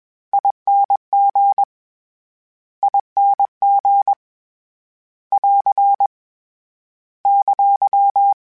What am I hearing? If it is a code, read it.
Morse code, heard as a keyed tone, about 21 wpm, 800 Hz. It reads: ING ING <AR> !
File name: General / General-mp3s / General @20